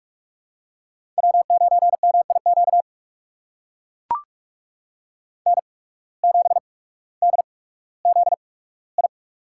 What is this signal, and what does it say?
Morse code, heard as a keyed tone, about 44 wpm, spaced as for 19 wpm, 690 Hz. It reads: W9MIQ N7DZI
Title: USA Call Signs - 5 Characters - 8x Character Spacing - 45wpm